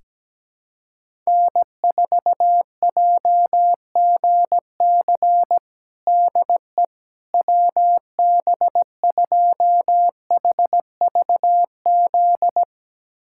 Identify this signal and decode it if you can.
Morse code, heard as a keyed tone, about 17 wpm, 700 Hz. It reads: N4JGC DE WB2HVZ
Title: USA Callsign DE Callsign 17wpm